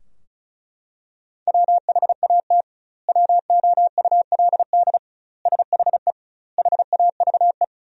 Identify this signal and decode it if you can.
Morse code, heard as a keyed tone, about 36 wpm, 700 Hz. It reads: WHAT WOULD SHE HAVE
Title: 2x Word-Spacing - Sentences from Top 100 Words - 35wpm